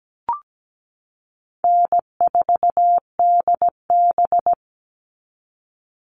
Morse code, heard as a keyed tone, about 17 wpm, 700 Hz. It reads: N4DB
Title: All Cut Numbers - 4-Digit 17wpm